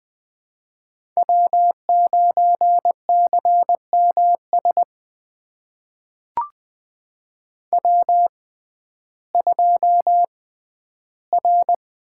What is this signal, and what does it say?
Morse code, heard as a keyed tone, about 20 wpm, spaced as for 12 wpm, 700 Hz. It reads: W9CMS W2R
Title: USA Call Signs - 5 Characters - 6x Character Spacing - 20wpm